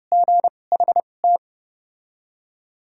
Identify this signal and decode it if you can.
Morse code, heard as a keyed tone, about 30 wpm, spaced as for 22 wpm, 700 Hz. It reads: GHT
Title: 2x Character Spacing - Top 500 Words - 30wpm